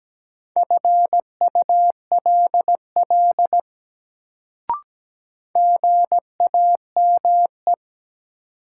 Morse code, heard as a keyed tone, about 17 wpm, 700 Hz. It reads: FULL GAME
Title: Sentences from Top 500 Words - New Word Drill - 17wpm